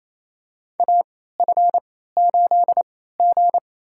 Morse code, heard as a keyed tone, about 28 wpm, spaced as for 16 wpm, 700 Hz. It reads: AF8G
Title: USA Call Signs - 3x Character Spacing - 28wpm